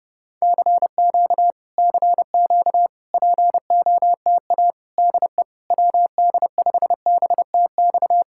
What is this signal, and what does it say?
Morse code, heard as a keyed tone, about 29 wpm, 700 Hz. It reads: CQ CQ POTA DE WD5BTX